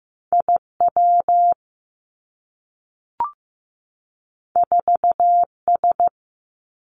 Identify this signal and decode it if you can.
Morse code, heard as a keyed tone, about 15 wpm, 700 Hz. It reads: I W 4 S
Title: Two Challenging Letter-Number Combinations 15wpm